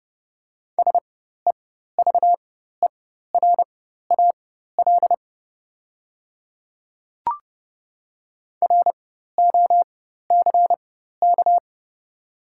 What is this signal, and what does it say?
Morse code, heard as a keyed tone, about 30 wpm, spaced as for 14 wpm, 690 Hz. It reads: SEVERAL ROCK
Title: 4x Character Spacing - Top 500 Words - 30wpm